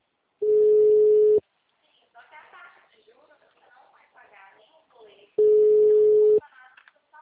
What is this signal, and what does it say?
Ringback tone (Europe, 425 Hz), with speech.